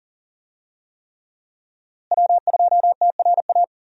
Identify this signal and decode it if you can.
Morse code, heard as a keyed tone, about 40 wpm, 700 Hz. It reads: W2TRA
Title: USA Call Signs - 5 Characters - Repeated 3x - 40wpm